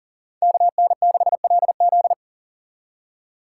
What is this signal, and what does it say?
Morse code, heard as a keyed tone, about 39 wpm, 700 Hz. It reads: KN6LZ
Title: USA Call Signs - 2x Character Spacing - 40wpm